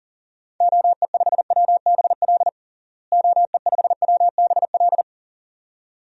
Morse code, heard as a keyed tone, about 40 wpm, 700 Hz. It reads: OE5WBL OE5WBL
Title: DXCC-100 Call Signs - 6 Characters - Repeated 2x - 40wpm